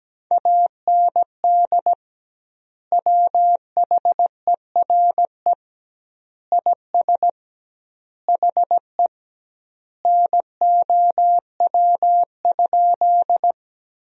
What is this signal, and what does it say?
Morse code, heard as a keyed tone, about 17 wpm, 700 Hz. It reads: AND WHERE IS HE NOW?
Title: 2x Word-Spacing - Sentences from Top 200 Words - 17wpm